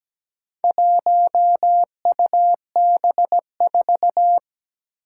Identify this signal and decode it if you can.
Morse code, heard as a keyed tone, about 17 wpm, 700 Hz. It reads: 1UB4
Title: Four Challenging Letter-Number Combinations 17wpm